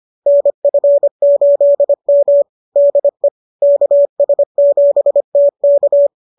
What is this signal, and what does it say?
Morse code, heard as a keyed tone, about 25 wpm, 570 Hz. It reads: NF8M DE KS7T K
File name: qso221 25